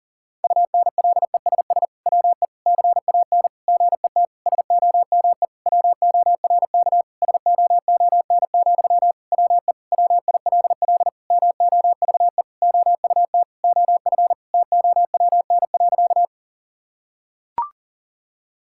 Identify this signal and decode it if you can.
Morse code, heard as a keyed tone, about 40 wpm, 700 Hz. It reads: UNLESS WE CAN GET SOME WORK SOON, WE WILL MOVE OUT OF TOWN.
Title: Sentences from Top 1000 Words - Review Part 6 - 40wpm